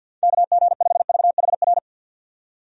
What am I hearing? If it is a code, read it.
Morse code, heard as a keyed tone, about 51 wpm, 700 Hz. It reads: KG5VHR
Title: USA Call Signs - 6 Characters - 50wpm